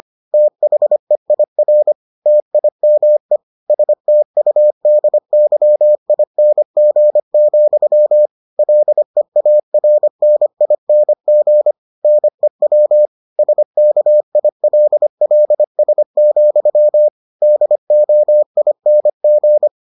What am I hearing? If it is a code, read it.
Morse code, heard as a keyed tone, about 25 wpm, 600 Hz. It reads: THEIR TIME STUDYING, LEARNING NEW SKILLS, DOING